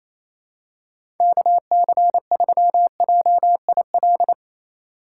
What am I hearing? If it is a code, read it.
Morse code, heard as a keyed tone, about 28 wpm, 700 Hz. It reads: KC3JIL
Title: USA Call Signs - 4x Character Spacing - 28wpm